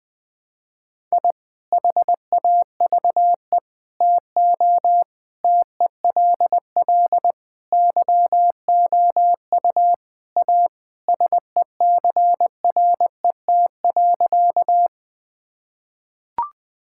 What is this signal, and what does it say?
Morse code, heard as a keyed tone, about 20 wpm, 700 Hz. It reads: I HAVE TO TELL YOU A SECRET.
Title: Sentences with 3-Word Phrases - Spoken Beginning - 20wpm